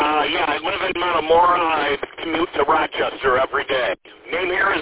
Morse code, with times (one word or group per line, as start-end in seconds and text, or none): none